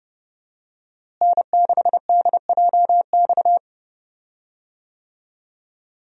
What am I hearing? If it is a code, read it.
Morse code, heard as a keyed tone, about 32 wpm, 700 Hz. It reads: N6DJX
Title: USA Call Signs - 5 Characters - Repeated 2x - 30wpm